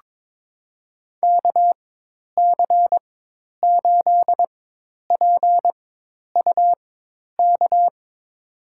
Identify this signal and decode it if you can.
Morse code, heard as a keyed tone, about 22 wpm, spaced as for 10 wpm, 700 Hz. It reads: KC8PUK
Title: USA Call Signs - 6 Characters - 4x Character Spacing - 22wpm